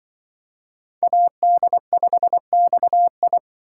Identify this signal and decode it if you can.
Morse code, heard as a keyed tone, about 24 wpm, 700 Hz. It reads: AD5XI
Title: Speed Racing - USA 2x2 Call Signs 20wpm